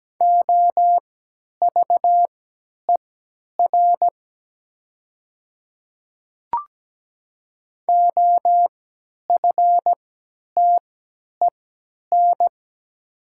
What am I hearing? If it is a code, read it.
Morse code, heard as a keyed tone, about 17 wpm, spaced as for 10 wpm, 700 Hz. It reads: OVER OFTEN
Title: 3x Character Spacing - Top 500 Words - 17wpm